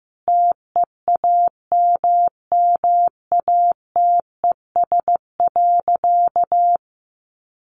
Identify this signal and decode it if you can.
Morse code, heard as a keyed tone, about 15 wpm, 700 Hz. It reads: TEAMMATES.
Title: Easy Phrases 15wpm